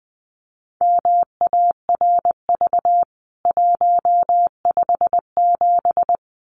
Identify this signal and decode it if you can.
Morse code, heard as a keyed tone, about 20 wpm, 700 Hz. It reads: MARV 157
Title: CWT Exchange 20wpm